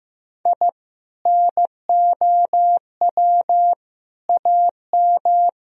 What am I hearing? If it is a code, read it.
Morse code, heard as a keyed tone, about 15 wpm, 700 Hz. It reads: I NOW AM
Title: Sentences from Top 1000 Words - Review Part 2 - 15wpm